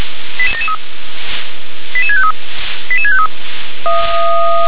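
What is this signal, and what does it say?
Five-tone selective call (ZVEI), heard as quick runs of five tones, tones of about 70 ms. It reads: 88773 88653 88653
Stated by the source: ZVEI five-tone call received by SDR.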